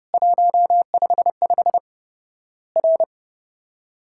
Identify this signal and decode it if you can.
Morse code, heard as a keyed tone, about 32 wpm, 700 Hz. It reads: 155 R
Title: Worked All Europe DX Contest - QTC Exchange - 30wpm